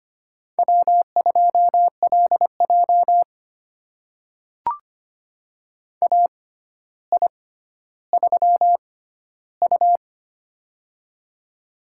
Morse code, heard as a keyed tone, about 25 wpm, spaced as for 13 wpm, 700 Hz. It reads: W2LJ AI3U
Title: USA Call Signs - 4 Characters - 6x Character Spacing - 25wpm